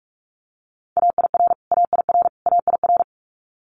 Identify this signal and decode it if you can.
Morse code, heard as a keyed tone, about 46 wpm, 700 Hz. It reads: AIR AIR AIR
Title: Multi-letter Phonemes Repeated 3x 45wpm